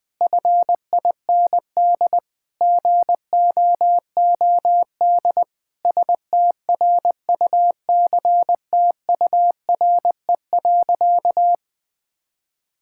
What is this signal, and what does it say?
Morse code, heard as a keyed tone, about 20 wpm, 700 Hz. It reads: FIND GOOD STRUCTURE.
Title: Sentences from Top 1000 Words - Review Part 6 - 20wpm